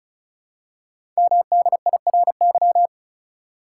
Morse code, heard as a keyed tone, about 36 wpm, 700 Hz. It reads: MDIRY
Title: Code Groups - 5 Letters - 35wpm